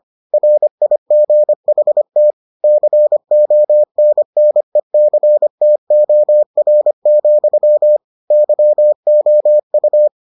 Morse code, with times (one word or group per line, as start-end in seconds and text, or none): 0.3-2.3 RIGHT
2.6-8.0 CONNECTOR,
8.3-10.1 YOU